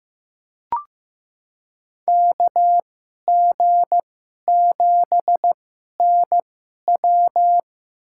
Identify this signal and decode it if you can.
Morse code, heard as a keyed tone, about 15 wpm, spaced as for 11 wpm, 700 Hz. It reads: KG7NW
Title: USA Call Signs - 5 Characters - 2x Character Spacing - 15wpm